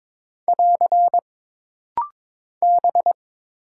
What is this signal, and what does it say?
Morse code, heard as a keyed tone, about 22 wpm, 700 Hz. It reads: <AR> B
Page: This rapid-fire timing is geared to push your ICR (Instant Character Recognition) skills.